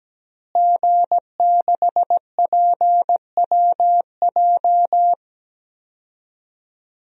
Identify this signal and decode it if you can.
Morse code, heard as a keyed tone, about 17 wpm, 700 Hz. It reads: G6PWJ
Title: DXCC-100 Call Signs - 5 Characters - Repeated 3x - 17wpm